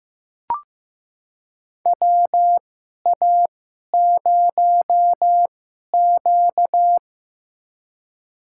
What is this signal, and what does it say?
Morse code, heard as a keyed tone, about 15 wpm, spaced as for 11 wpm, 700 Hz. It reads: WA0Q
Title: USA Call Signs - 4 Characters - 2x Character Spacing - 15wpm